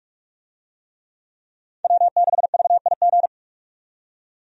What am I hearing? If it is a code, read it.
Morse code, heard as a keyed tone, about 43 wpm, 700 Hz. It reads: W6VIG
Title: USA Call Signs - 5 Characters - 45wpm